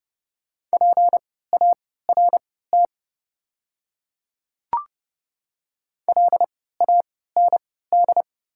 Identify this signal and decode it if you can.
Morse code, heard as a keyed tone, about 30 wpm, spaced as for 17 wpm, 690 Hz. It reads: PART LAND
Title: 3x Character Spacing - Top 200 Words - 30wpm